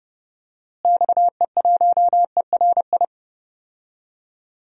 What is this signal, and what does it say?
Morse code, heard as a keyed tone, about 30 wpm, 700 Hz. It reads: XE1ERI
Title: DXCC-100 Call Signs - 6 Characters - Repeated 3x - 30wpm